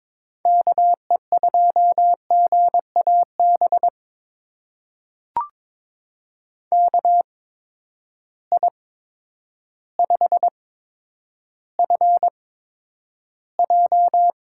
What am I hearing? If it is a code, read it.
Morse code, heard as a keyed tone, about 22 wpm, spaced as for 10 wpm, 700 Hz. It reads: KE2GAB KI5FJ
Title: USA Call Signs - 6 Characters - 8x Character Spacing - 22wpm